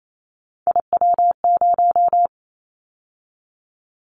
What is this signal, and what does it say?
Morse code, heard as a keyed tone, about 28 wpm, 700 Hz. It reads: IW0
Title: Callsign Prefixes for Top 25 DXCC - 28wpm